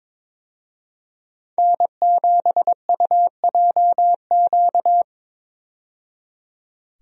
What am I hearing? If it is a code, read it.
Morse code, heard as a keyed tone, about 22 wpm, 700 Hz. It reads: N7UJQ
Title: USA Call Signs - 5 Characters - 22wpm